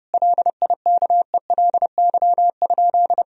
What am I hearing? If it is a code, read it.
Morse code, heard as a keyed tone, about 32 wpm, 700 Hz. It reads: LIKELY?
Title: Sentences from Top 1000 Words - Review Part 4 - 30wpm